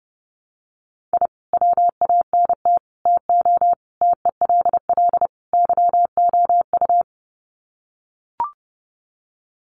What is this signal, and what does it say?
Morse code, heard as a keyed tone, about 30 wpm, 700 Hz. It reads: I WANT TO TELL YOU
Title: Sets of 5 Words Encore 30wpm